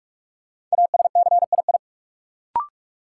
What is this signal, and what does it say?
Morse code, heard as a keyed tone, about 43 wpm, 700 Hz. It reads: ASCII
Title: Sets of 5 Words - New Word Drill - 45wpm